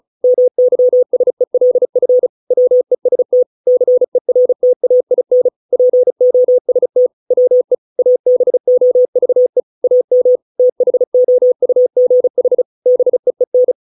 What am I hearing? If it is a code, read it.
Morse code, heard as a keyed tone, about 36 wpm, 500 Hz. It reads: MYSELF WEST CERTAIN POST WE ABOVE AM THOUGH BEEN